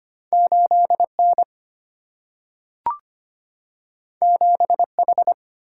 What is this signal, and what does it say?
Morse code, heard as a keyed tone, about 25 wpm, 700 Hz. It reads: 8 N 7 H
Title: Single-Letter Single-Number 25wpm